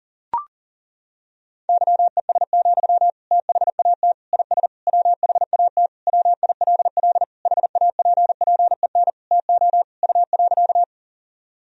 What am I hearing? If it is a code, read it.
Morse code, heard as a keyed tone, about 40 wpm, 700 Hz. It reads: YES, THAT IS WHAT WILL HAPPEN TO U.
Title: Sentences from Top 500 Words and Abbreviations 40wpm